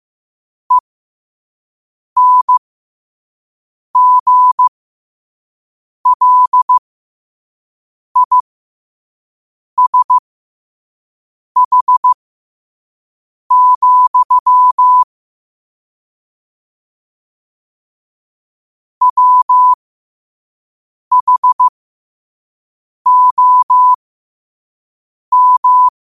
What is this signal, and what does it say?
Morse code, heard as a keyed tone, about 15 wpm, spaced as for 5 wpm, 1000 Hz. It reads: ENGLISH, WHOM
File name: qotdfiles / 5wpm